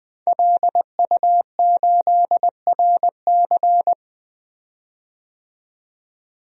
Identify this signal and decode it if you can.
Morse code, heard as a keyed tone, about 21 wpm, 700 Hz. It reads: LU8RC
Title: DXCC-100 Call Signs - 5 Characters - 20wpm